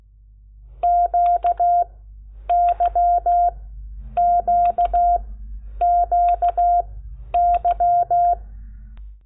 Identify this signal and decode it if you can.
Morse code, heard as a keyed tone, about 16 wpm, spaced as for 9 wpm, 700 Hz. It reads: QYQQY
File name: Lecciones